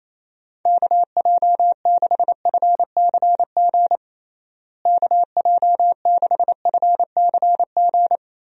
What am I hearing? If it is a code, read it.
Morse code, heard as a keyed tone, about 28 wpm, 700 Hz. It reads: KJ6FCG KJ6FCG
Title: USA Call Signs - 6 Characters - Repeated 3x - 28wpm